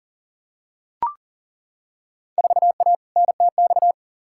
Morse code, heard as a keyed tone, about 40 wpm, 700 Hz. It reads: 4A NTX
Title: Field Day Exchanges - 40wpm